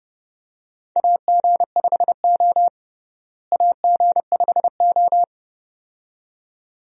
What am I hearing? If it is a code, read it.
Morse code, heard as a keyed tone, about 29 wpm, 700 Hz. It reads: AG5O AG5O
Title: USA Call Signs - 4 Characters - Repeated 2x - 30wpm